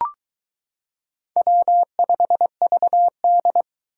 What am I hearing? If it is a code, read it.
Morse code, heard as a keyed tone, about 23 wpm, 700 Hz. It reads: W5VD